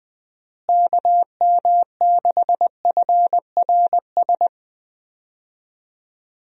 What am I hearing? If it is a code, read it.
Morse code, heard as a keyed tone, about 21 wpm, 700 Hz. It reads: KM6FRS
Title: USA Call Signs - 6 Characters - Repeated 3x - 20wpm